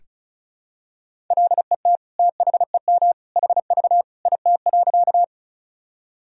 Morse code, heard as a keyed tone, about 34 wpm, 700 Hz. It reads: LET THEM HV IT.
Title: Sentences from Top 300 Words and Abbreviations 35wpm